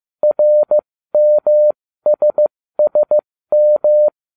Morse code, heard as a keyed tone, about 15 wpm, spaced as for 13 wpm, 600 Hz. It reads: RMSSM